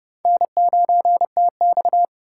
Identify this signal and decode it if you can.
Morse code, heard as a keyed tone, about 30 wpm, 700 Hz. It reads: N9TX
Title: USA Call Signs - 3x Character Spacing - 30wpm